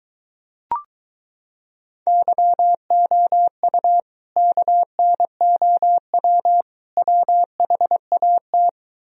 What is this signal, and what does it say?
Morse code, heard as a keyed tone, about 23 wpm, 700 Hz. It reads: YOU KNOW WHAT